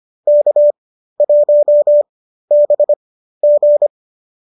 Morse code, heard as a keyed tone, about 25 wpm, spaced as for 13 wpm, 570 Hz. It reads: K1BG